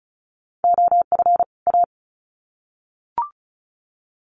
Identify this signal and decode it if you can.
Morse code, heard as a keyed tone, about 34 wpm, 700 Hz. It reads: OF A
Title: Sets of 2 Words 35wpm